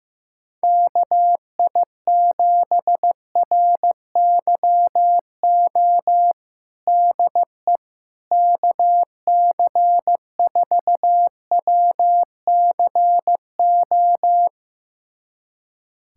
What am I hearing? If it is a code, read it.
Morse code, heard as a keyed tone, about 16 wpm, 700 Hz. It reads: KI7RYO DE KC4WCO
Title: USA Callsign DE Callsign - 6 Characters - 15wpm